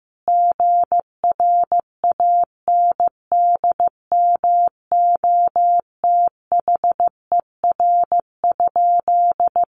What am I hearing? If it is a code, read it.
Morse code, heard as a keyed tone, about 15 wpm, 700 Hz. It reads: GRANDMOTHER?
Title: Sentences from Top 1200 Words 15wpm